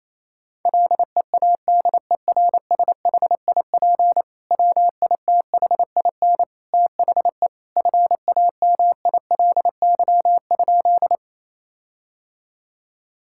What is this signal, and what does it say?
Morse code, heard as a keyed tone, about 28 wpm, 700 Hz. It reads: LEADERSHIP WITHIN THE FAMILY?
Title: Sentences from Top 1600 Words 28wpm